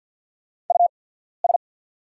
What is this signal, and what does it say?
Morse code, heard as a keyed tone, about 48 wpm, spaced as for 14 wpm, 690 Hz. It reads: US
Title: 8x Character Spacing - Top 300 Words - 50wpm